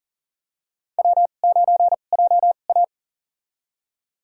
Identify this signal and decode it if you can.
Morse code, heard as a keyed tone, about 39 wpm, spaced as for 29 wpm, 690 Hz. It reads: W9JA
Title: USA Call Signs - 4 Characters - 2x Character Spacing - 40wpm